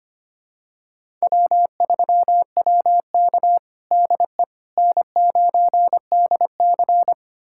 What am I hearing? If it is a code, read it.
Morse code, heard as a keyed tone, about 25 wpm, 700 Hz. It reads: W3WK DE N9DC
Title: USA Callsign DE Callsign - 4 Characters - 25wpm